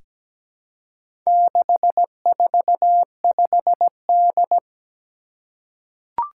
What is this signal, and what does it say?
Morse code, heard as a keyed tone, about 17 wpm, 700 Hz. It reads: 645D
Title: Four Challenging Letter-Number Combinations 17wpm